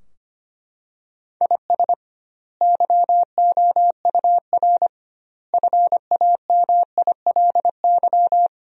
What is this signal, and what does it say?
Morse code, heard as a keyed tone, about 25 wpm, 700 Hz. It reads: IS YOUR FAMILY
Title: 2x Word-Spacing - Sentences from Top 400 Words - 25wpm